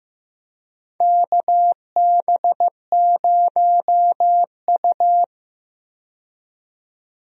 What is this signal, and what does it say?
Morse code, heard as a keyed tone, about 15 wpm, 700 Hz. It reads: KB0U
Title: USA Call Signs - 4 Characters - 15wpm